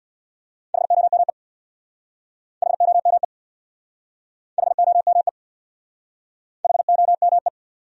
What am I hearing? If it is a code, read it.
Morse code, heard as a keyed tone, about 64 wpm, 700 Hz. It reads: HOME HOME HOME HOME